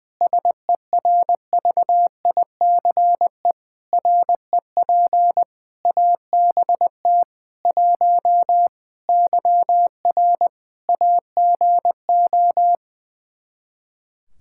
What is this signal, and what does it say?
Morse code, heard as a keyed tone, about 20 wpm, 700 Hz. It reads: SERVICE REP ABT 1 YR AGO
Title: Practice QSO - Exchange 3 - More Info - 20wpm